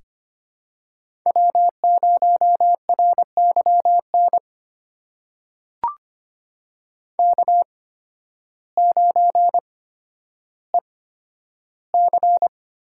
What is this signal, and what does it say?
Morse code, heard as a keyed tone, about 25 wpm, spaced as for 12 wpm, 700 Hz. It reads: W0RYN K9EC
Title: USA Call Signs - 8x Character Spacing - 25wpm